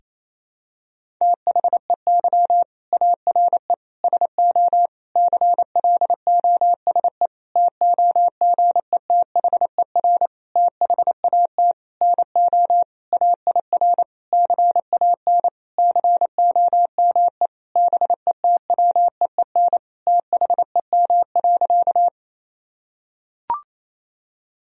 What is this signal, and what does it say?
Morse code, heard as a keyed tone, about 28 wpm, 700 Hz. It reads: THEY ARE SO CLOSE TOGETHER THAT NO AIR CAN COME BETWEEN THEM.
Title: Sentences from Top 1000 Words - Review Part 1 - 28wpm